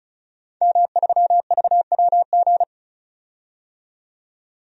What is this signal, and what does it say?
Morse code, heard as a keyed tone, about 34 wpm, 700 Hz. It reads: M3VWG
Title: DXCC-100 Call Signs - 5 Characters - 35wpm